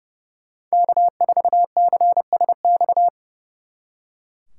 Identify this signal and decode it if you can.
Morse code, heard as a keyed tone, about 36 wpm, 700 Hz. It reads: K4CSX
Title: Speed Racing - USA 1x3 Call Signs 30wpm